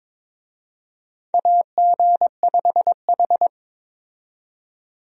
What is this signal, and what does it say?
Morse code, heard as a keyed tone, about 22 wpm, 700 Hz. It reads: AG5H
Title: USA Call Signs - 4 Characters - 22wpm